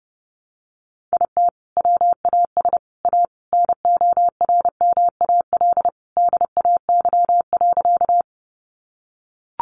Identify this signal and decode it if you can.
Morse code, heard as a keyed tone, about 30 wpm, 700 Hz. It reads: IT WAS A NORMAL DAY.
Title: Sentences from Top 1200 Words 30wpm